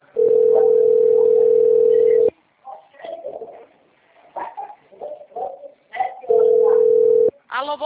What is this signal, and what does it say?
Ringback tone (North America), 440 + 480 Hz.